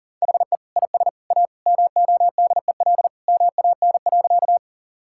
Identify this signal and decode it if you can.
Morse code, heard as a keyed tone, about 37 wpm, 700 Hz. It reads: HE IS A MODEL MAN.
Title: Sentences from Top 1000 Words - Review Part 5 - 40wpm